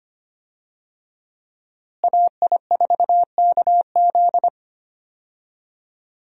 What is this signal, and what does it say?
Morse code, heard as a keyed tone, about 25 wpm, 700 Hz. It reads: AI4KZ
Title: USA Call Signs - 5 Characters - Repeated 3x - 25wpm